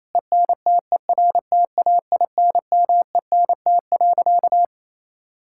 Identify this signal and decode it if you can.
Morse code, heard as a keyed tone, about 28 wpm, 700 Hz. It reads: ENTERTAINMENT.
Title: Sentences from Top 1000 Words - Review Part 6 - 28wpm